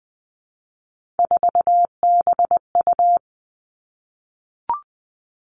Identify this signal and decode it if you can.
Morse code, heard as a keyed tone, about 20 wpm, 700 Hz. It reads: 4BU
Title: Three Challenging Letter-Number Combinations 20wpm